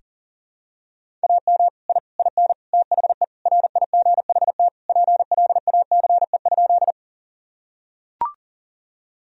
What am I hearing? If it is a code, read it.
Morse code, heard as a keyed tone, about 40 wpm, 700 Hz. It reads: AM I IN THE RIGHT PLACE?
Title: Sentences from Top 1000 Words - Review Part 2 - 40wpm